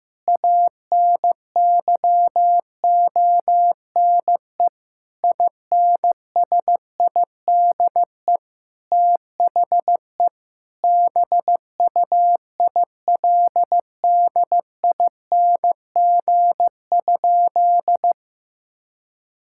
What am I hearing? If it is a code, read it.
Morse code, heard as a keyed tone, about 15 wpm, 700 Hz. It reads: ANYONE INSIDE THE BUILDING?
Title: Sentences from Top 1000 Words - Review Part 6 - 15wpm